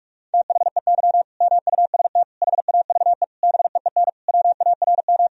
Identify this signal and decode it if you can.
Morse code, heard as a keyed tone, about 41 wpm, 700 Hz. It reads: THEY MUST HAVE BEEN WARM
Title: Sentences from Top 1000 Words - Review Part 2 - 45wpm